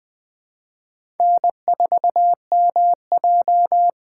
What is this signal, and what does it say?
Morse code, heard as a keyed tone, about 20 wpm, 700 Hz. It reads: N4MJ
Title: 2023 SOTA Chaser Call Signs - 20wpm